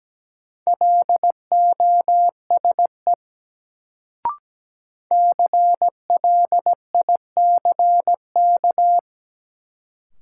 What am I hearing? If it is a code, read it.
Morse code, heard as a keyed tone, about 17 wpm, 700 Hz. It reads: LOSE CLICK
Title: Sentences from Top 1100 Words - New Word Drill Alt - 17wpm